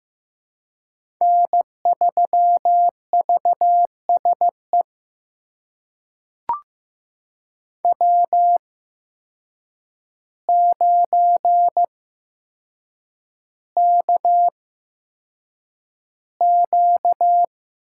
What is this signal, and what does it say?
Morse code, heard as a keyed tone, about 15 wpm, spaced as for 7 wpm, 700 Hz. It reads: N3VSE W9KQ
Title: USA Call Signs - 8x Character Spacing - 15wpm